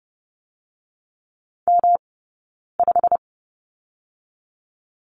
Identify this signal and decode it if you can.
Morse code, heard as a keyed tone, about 30 wpm, 700 Hz. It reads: M 5
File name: Lesson 18d - Letter m - Two Characters 30wpm